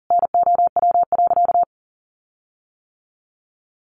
Morse code, heard as a keyed tone, about 40 wpm, 700 Hz. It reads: NOW.
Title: Sentences from Top 1400 Words 40wpm